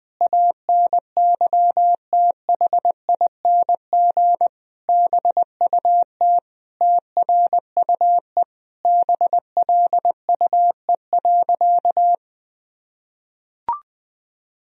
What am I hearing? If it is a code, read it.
Morse code, heard as a keyed tone, about 20 wpm, 700 Hz. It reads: ANYTHING BUT TRUE BLUE.
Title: Sentences from Top 1000 Words - Review Part 3 - 20wpm